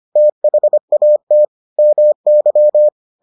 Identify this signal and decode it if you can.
Morse code, heard as a keyed tone, about 25 wpm, 600 Hz. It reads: THAT MY